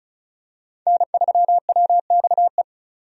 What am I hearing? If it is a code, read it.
Morse code, heard as a keyed tone, about 34 wpm, 700 Hz. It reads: N3WXE
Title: USA Call Signs - 5 Characters - 35wpm